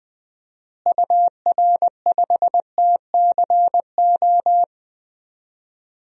Morse code, heard as a keyed tone, about 20 wpm, 700 Hz. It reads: UR5TCO
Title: DXCC-100 Call Signs - 6 Characters - Repeated 2x - 20wpm